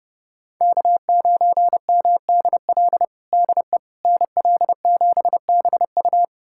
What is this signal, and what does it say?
Morse code, heard as a keyed tone, about 30 wpm, 700 Hz. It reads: K9MDL DE NL7BU
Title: USA Callsign DE Callsign - 5 Characters - 30wpm